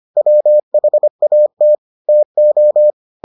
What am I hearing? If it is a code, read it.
Morse code, heard as a keyed tone, about 25 wpm, 600 Hz. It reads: WHAT TO